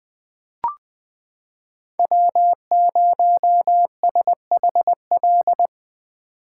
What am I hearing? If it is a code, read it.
Morse code, heard as a keyed tone, about 20 wpm, 700 Hz. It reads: W0SHL
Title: K1USN SST QSO Practice - 20wpm